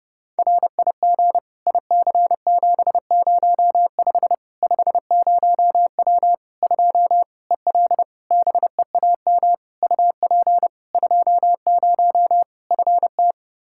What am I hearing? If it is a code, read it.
Morse code, heard as a keyed tone, about 30 wpm, 700 Hz. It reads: RIG IC705 50W 2 EL BEAM UP 20 FT
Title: Practice QSO - Exchange 2 - Info - 30wpm